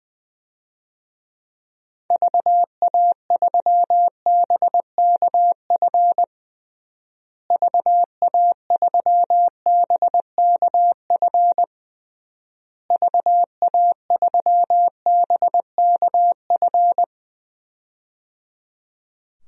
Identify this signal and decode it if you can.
Morse code, heard as a keyed tone, about 21 wpm, 700 Hz. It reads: VA3BKF VA3BKF VA3BKF
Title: DXCC-100 Call Signs - 6 Characters - Repeated 3x - 20wpm